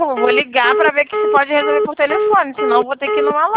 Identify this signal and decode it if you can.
Congestion tone (425 Hz), with speech.